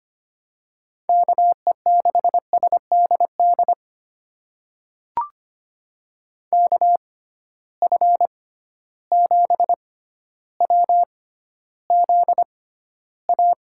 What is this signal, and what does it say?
Morse code, heard as a keyed tone, about 25 wpm, spaced as for 13 wpm, 700 Hz. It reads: KE6SDD KF7WZA
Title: USA Call Signs - 6 Characters - 6x Character Spacing - 25wpm